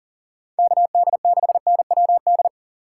Morse code, heard as a keyed tone, about 48 wpm, 700 Hz. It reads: KD6NWD
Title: Speed Racing - USA 2x3 Call Signs 40wpm